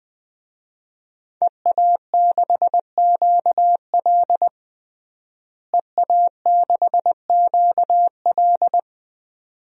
Morse code, heard as a keyed tone, about 20 wpm, 700 Hz. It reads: EA6QL EA6QL
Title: DXCC-100 Call Signs - 5 Characters - Repeated 2x - 20wpm